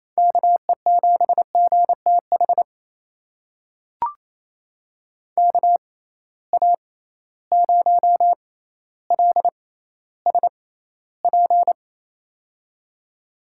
Morse code, heard as a keyed tone, about 28 wpm, spaced as for 14 wpm, 700 Hz. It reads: KE7GTH KA0LSP
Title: USA Call Signs - 6 Characters - 6x Character Spacing - 28wpm